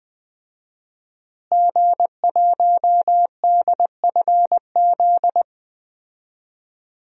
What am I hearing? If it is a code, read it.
Morse code, heard as a keyed tone, about 20 wpm, 700 Hz. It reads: G1DFZ
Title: DXCC-100 Call Signs - 5 Characters - Repeated 3x - 20wpm